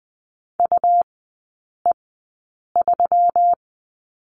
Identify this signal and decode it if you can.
Morse code, heard as a keyed tone, about 20 wpm, spaced as for 8 wpm, 700 Hz. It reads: UE3
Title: 121 Lesson 20e - Number 3 - Three Characters 20wpm